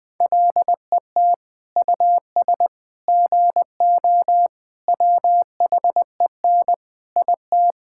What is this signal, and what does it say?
Morse code, heard as a keyed tone, about 20 wpm, 700 Hz. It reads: LET US GO WHEN IT
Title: Sentences with 2-Word Phrases - 20wpm